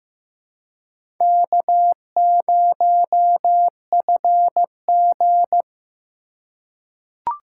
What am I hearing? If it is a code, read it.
Morse code, heard as a keyed tone, about 15 wpm, 700 Hz. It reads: K0FG
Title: USA Call Signs - 4 Characters - 3x Character Spacing - 15wpm